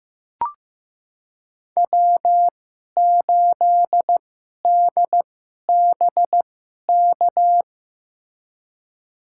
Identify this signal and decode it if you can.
Morse code, heard as a keyed tone, about 15 wpm, spaced as for 11 wpm, 700 Hz. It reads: W8DBK
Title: USA Call Signs - 5 Characters - 2x Character Spacing - 15wpm